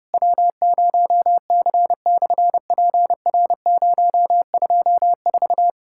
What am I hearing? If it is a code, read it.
Morse code, heard as a keyed tone, about 30 wpm, 700 Hz. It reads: W0C/PR024
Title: SOTA QSO Practice 30wpm